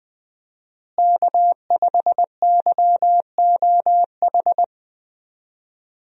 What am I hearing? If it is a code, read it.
Morse code, heard as a keyed tone, about 20 wpm, 700 Hz. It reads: K5YOH
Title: USA Call Signs - 5 Characters - Repeated 2x - 20wpm